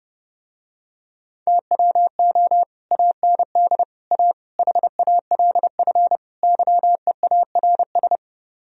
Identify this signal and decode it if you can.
Morse code, heard as a keyed tone, about 30 wpm, 700 Hz. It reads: TWO AND A HALF YEARS